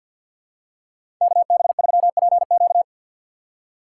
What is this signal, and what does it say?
Morse code, heard as a keyed tone, about 46 wpm, 700 Hz. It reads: KB3PQ
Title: USA Call Signs - 5 Characters - 6x Character Spacing - 50wpm